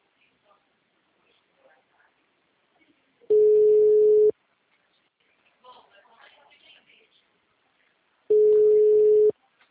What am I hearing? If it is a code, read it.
Ringback tone (Europe, 425 Hz), with speech.
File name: out- -2009- - - .45871